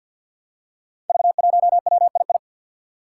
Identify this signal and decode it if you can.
Morse code, heard as a keyed tone, about 47 wpm, 700 Hz. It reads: V1WII
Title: Five Challenging Letter-Number Combinations 50wpm